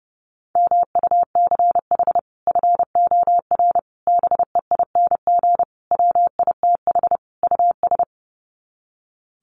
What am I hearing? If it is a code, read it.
Morse code, heard as a keyed tone, about 30 wpm, 700 Hz. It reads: MUCH FOR BEING WITH US
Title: Sets of 5 Words 30wpm